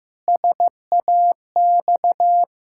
Morse code, heard as a keyed tone, about 15 wpm, 700 Hz. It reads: SAX
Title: Pangram Challenge v2 - 15wpm